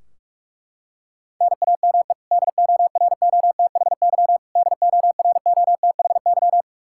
Morse code, heard as a keyed tone, about 41 wpm, 700 Hz. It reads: NAME DOROTHY DOROTHY
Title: Practice QSO - Exchange 1 - Beginning - 45wpm